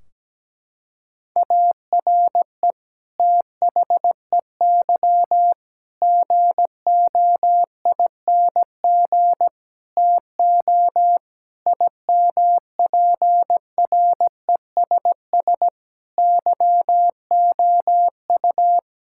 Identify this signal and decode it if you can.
Morse code, heard as a keyed tone, about 17 wpm, 700 Hz. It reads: ARE THEY GOING TO IMPRESS YOU
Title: Sentences from Top 1600 Words 17wpm